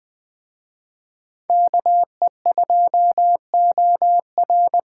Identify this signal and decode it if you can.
Morse code, heard as a keyed tone, about 21 wpm, 700 Hz. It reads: KE2OR
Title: Speed Racing - USA 2x2 Call Signs 17wpm